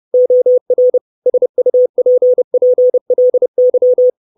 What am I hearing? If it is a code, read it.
Morse code, heard as a keyed tone, about 30 wpm, 500 Hz. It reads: OR SUPPLY
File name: 632 Words 30WPM